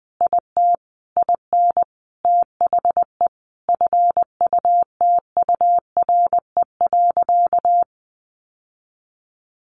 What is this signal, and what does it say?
Morse code, heard as a keyed tone, about 20 wpm, 700 Hz. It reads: IT IN THE FUTURE.
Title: Sentences from Top 1200 Words 20wpm